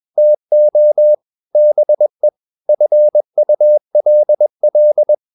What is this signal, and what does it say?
Morse code, heard as a keyed tone, about 21 wpm, 600 Hz. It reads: TO BE FULL
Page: CW med Gnister